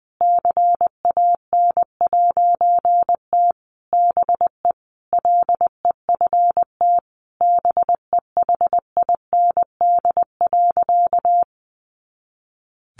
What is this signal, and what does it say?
Morse code, heard as a keyed tone, about 20 wpm, 700 Hz. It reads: CAN'T BE LEFT BEHIND.
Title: A Sound Of Thunder 20wpm